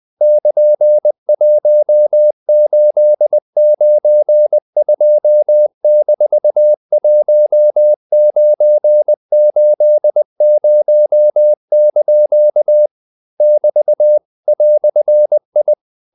Morse code, heard as a keyed tone, about 20 wpm, 600 Hz. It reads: <KN>1892-1980) <BT> "I
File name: CW / Practice / EnglishQuotations0320